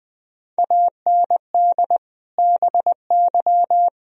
Morse code, heard as a keyed tone, about 20 wpm, 700 Hz. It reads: AND BY
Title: Sentences from Top 1000 Words - Review Part 3 - 20wpm